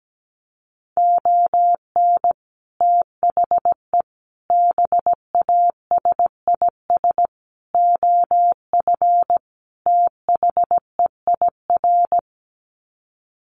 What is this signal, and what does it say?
Morse code, heard as a keyed tone, about 17 wpm, 700 Hz. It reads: ON THE BASIS OF THEIR
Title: Sets of 5 Words Encore 17wpm